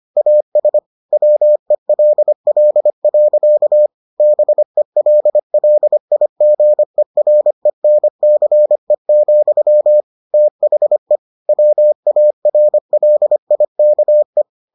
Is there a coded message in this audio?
Yes: Morse code from 0.2 to 14.4 s, heard as a keyed tone, about 25 wpm, 600 Hz. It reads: AS WELL. BELLIGERENCE, THE WARLIKE